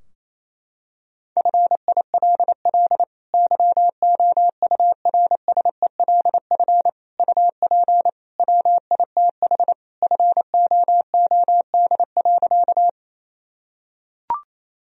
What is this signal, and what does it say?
Morse code, heard as a keyed tone, about 27 wpm, 700 Hz. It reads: FILL YOURSELF UP WITH FOOD.
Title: Sentences from Top 1000 Words - Review Part 3 - 28wpm